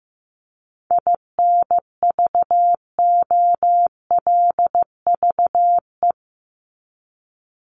Morse code, heard as a keyed tone, about 15 wpm, 700 Hz. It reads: INVOLVE
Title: ED Suffix - Top 100 Words with Base Words 15wpm